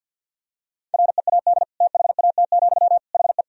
Morse code, heard as a keyed tone, about 51 wpm, 700 Hz. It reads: READ THAT, HE
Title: Sentences from Top 200 Words and Abbreviations 50wpm